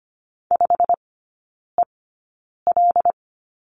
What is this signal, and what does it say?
Morse code, heard as a keyed tone, about 25 wpm, spaced as for 9 wpm, 700 Hz. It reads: 5EL
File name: Lesson 17e - Letter u - Three Characters 25wpm